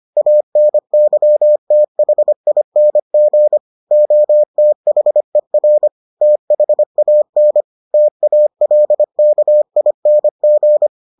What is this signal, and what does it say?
Morse code, heard as a keyed tone, about 25 wpm, 600 Hz. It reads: ANYTHING OTHER THAN TALKING